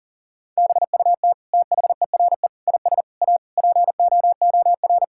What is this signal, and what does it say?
Morse code, heard as a keyed tone, about 40 wpm, 700 Hz. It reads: BUT THERE IS A POOR
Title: Sentences from Top 1000 Words - Review Part 6 - 40wpm